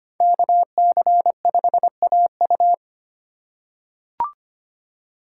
Morse code, heard as a keyed tone, about 25 wpm, 700 Hz. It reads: KC5AU
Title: USA Call Signs - 5 Characters - 2x Character Spacing - 25wpm